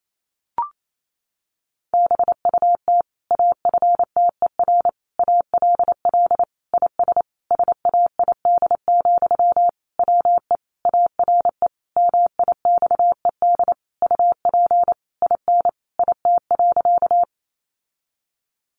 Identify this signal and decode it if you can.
Morse code, heard as a keyed tone, about 28 wpm, 700 Hz. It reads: BUT AFTER ALL IS SAID, WE ARE MIXED UP IN IT.
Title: Sentences from Top 1500 Words 28wpm